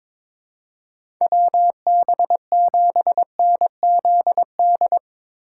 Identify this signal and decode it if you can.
Morse code, heard as a keyed tone, about 22 wpm, 700 Hz. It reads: WB7NZD
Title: USA Call Signs - 3x Character Spacing - 22wpm